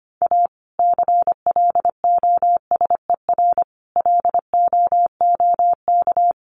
Morse code, heard as a keyed tone, about 25 wpm, 700 Hz. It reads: A CLOSER LOOK
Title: Sets of 5 Words Encore 25wpm